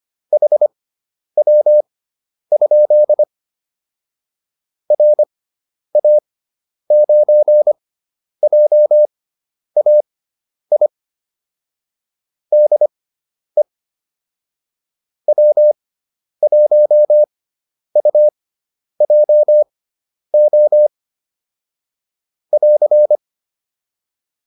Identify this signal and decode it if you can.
Morse code, heard as a keyed tone, about 25 wpm, spaced as for 10 wpm, 600 Hz. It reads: HW? RA9JAI DE W1UJO <AR>
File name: QSO401 10